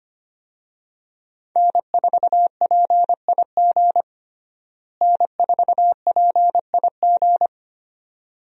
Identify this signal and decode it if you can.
Morse code, heard as a keyed tone, about 25 wpm, 700 Hz. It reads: N4PIG N4PIG
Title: USA Call Signs - 5 Characters - Repeated 2x - 25wpm